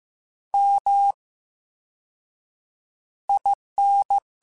Morse code, heard as a keyed tone, about 15 wpm, 800 Hz. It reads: M I N
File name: Special INGs-mp3s / Special INGs-mp3s / VerySlowToFaster Ing @13